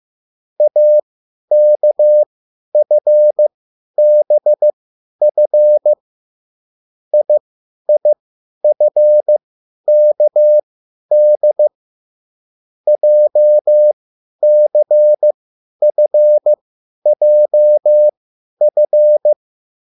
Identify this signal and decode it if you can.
Morse code, heard as a keyed tone, about 15 wpm, spaced as for 10 wpm, 600 Hz. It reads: AKFBF IIFKD JCFJF